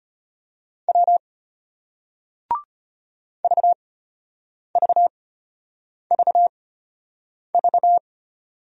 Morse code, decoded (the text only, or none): W VVVV